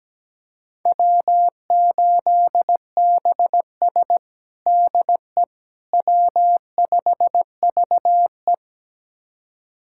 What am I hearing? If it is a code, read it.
Morse code, heard as a keyed tone, about 17 wpm, 700 Hz. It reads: W8BS DE W5VE
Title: USA Callsign DE Callsign - 4 Characters - 17wpm